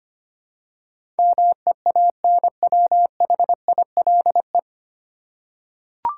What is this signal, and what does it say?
Morse code, heard as a keyed tone, about 25 wpm, 700 Hz. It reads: MEANWHILE
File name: practice / 25wpm / 9 Letter Words - First 3 Letters Given - 25wpm